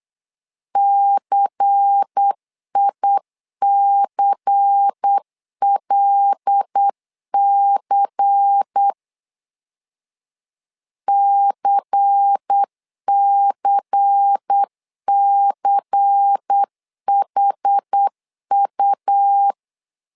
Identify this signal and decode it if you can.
Morse code, heard as a keyed tone, about 8 wpm, 800 Hz. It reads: CICLC CCCHU
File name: LEZIONI / 5 LBVC / 5 LBVC 8wpm 2935